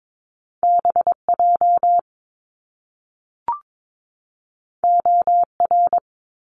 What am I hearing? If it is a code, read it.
Morse code, heard as a keyed tone, about 22 wpm, 700 Hz. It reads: BJ OR
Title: Top Two Letter Combinations in Words 22wpm